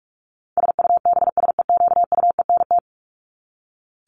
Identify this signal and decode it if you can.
Morse code, heard as a keyed tone, about 44 wpm, 700 Hz. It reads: SUBSEQUENT
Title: LY Suffix - Top 100 Words with Base Words 45wpm